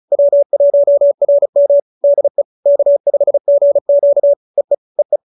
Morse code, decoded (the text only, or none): W1RM DE K5GQ EE EE